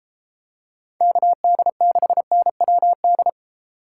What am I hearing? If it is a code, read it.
Morse code, heard as a keyed tone, about 33 wpm, 700 Hz. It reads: KD6NWD